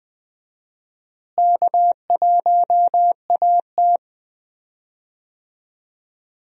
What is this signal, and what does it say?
Morse code, heard as a keyed tone, about 20 wpm, 700 Hz. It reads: K1AT
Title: USA Call Signs - 4 Characters - Repeated 3x - 20wpm